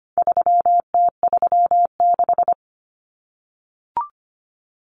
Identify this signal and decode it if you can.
Morse code, heard as a keyed tone, about 25 wpm, 700 Hz. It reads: 3T36
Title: Common Cut Numbers - 4-Digit 25wpm